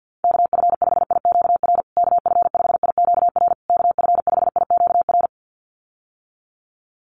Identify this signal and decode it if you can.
Morse code, heard as a keyed tone, about 50 wpm, 700 Hz. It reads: KF5IQR KF5IQR KF5IQR
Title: USA Call Signs Repeated 3x 50wpm